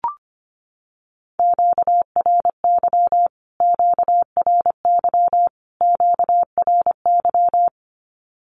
Morse code, heard as a keyed tone, about 25 wpm, 700 Hz. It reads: QRY QRY QRY
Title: Q-Codes Repeated 3x 25wpm